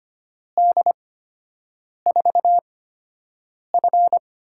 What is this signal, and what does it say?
Morse code, heard as a keyed tone, about 25 wpm, spaced as for 7 wpm, 700 Hz. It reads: D4F
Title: USA Call Signs - 8x Character Spacing - 25wpm